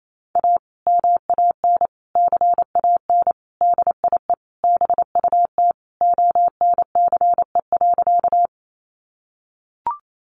Morse code, heard as a keyed tone, about 28 wpm, 700 Hz. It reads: A MAN CAN DIE BUT ONCE.
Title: Sentences from Top 1300 Words 28wpm